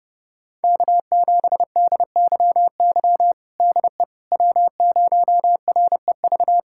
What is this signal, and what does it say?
Morse code, heard as a keyed tone, about 30 wpm, 700 Hz. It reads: K7DYY DE W0REV
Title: USA Callsign DE Callsign - 5 Characters - 30wpm